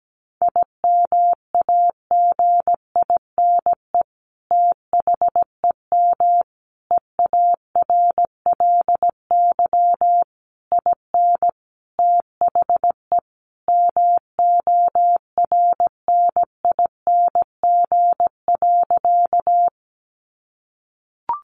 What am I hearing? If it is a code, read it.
Morse code, heard as a keyed tone, about 17 wpm, 700 Hz. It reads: IMAGINE THEM EARLY IN THE MORNING.
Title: Sentences from Top 1400 Words 17wpm